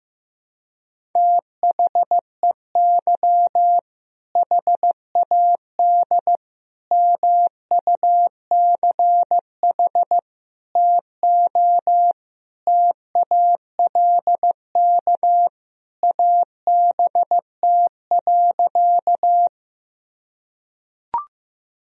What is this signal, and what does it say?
Morse code, heard as a keyed tone, about 15 wpm, 700 Hz. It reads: THEY HAD MUCH TO TALK ABT.
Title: Sentences from Top 400 Words and Abbreviations 15wpm